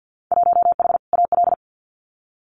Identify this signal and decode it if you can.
Morse code, heard as a keyed tone, about 50 wpm, 700 Hz. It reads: 1H AL
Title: Winter Field Day Exchanges 50wpm